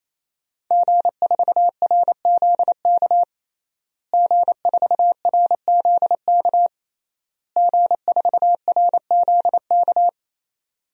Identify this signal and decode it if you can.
Morse code, heard as a keyed tone, about 28 wpm, 700 Hz. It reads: G4RZK G4RZK G4RZK
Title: DXCC-100 Call Signs - 5 Characters - Repeated 3x - 28wpm